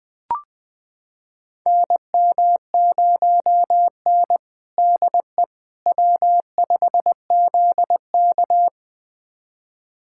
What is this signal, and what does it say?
Morse code, heard as a keyed tone, about 21 wpm, 700 Hz. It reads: NM0N DE W5ZK
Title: USA Callsign DE Callsign - 4 Characters - 20wpm